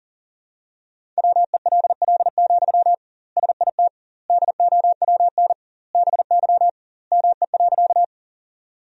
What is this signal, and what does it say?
Morse code, heard as a keyed tone, about 40 wpm, 700 Hz. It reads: WELL, SIT DOWN BY ME.
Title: 2x Word-Spacing - Sentences from Top 500 Words - 40wpm